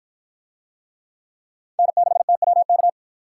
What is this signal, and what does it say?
Morse code, heard as a keyed tone, about 53 wpm, 700 Hz. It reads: N6TWK